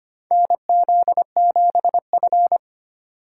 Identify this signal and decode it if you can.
Morse code, heard as a keyed tone, about 25 wpm, 700 Hz. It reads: NZ7F
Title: USA Call Signs - 4 Characters - 3x Character Spacing - 25wpm